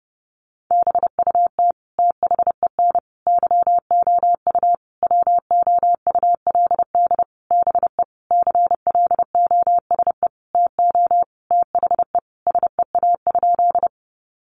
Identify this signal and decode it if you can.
Morse code, heard as a keyed tone, about 28 wpm, 700 Hz. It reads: BUT THEN YOU WOULD BE CLOSE TO THE SEA?
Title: Sentences from Top 300 Words 30wpm